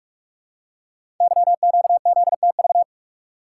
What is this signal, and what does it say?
Morse code, heard as a keyed tone, about 44 wpm, 700 Hz. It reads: YQZTV
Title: Code Groups - 5 Letter:Numbers - 45wpm